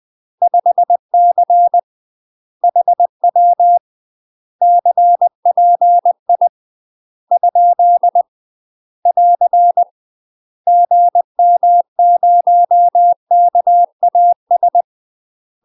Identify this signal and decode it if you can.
Morse code, heard as a keyed tone, about 20 wpm, 700 Hz. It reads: 5C HW CPI ? <AR> GM0KAS
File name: UK MASTER QSOs TRAINING / 20WPM / UK MASTER QSOs 0012